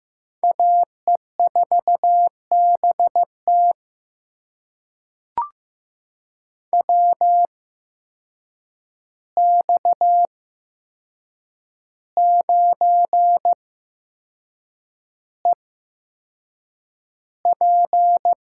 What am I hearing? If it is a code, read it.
Morse code, heard as a keyed tone, about 15 wpm, spaced as for 6 wpm, 700 Hz. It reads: AE4BT WX9EP
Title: USA Call Signs - 5 Characters - 8x Character Spacing - 15wpm